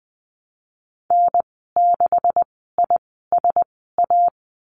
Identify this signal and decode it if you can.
Morse code, heard as a keyed tone, about 20 wpm, spaced as for 15 wpm, 700 Hz. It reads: N6ISA
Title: 128 Lesson 21g - Number 6 - Call Signs 20wpm